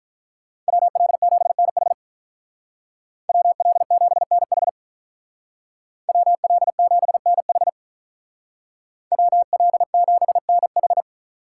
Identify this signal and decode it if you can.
Morse code, heard as a keyed tone, about 46 wpm, 700 Hz. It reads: WL7NH WL7NH WL7NH WL7NH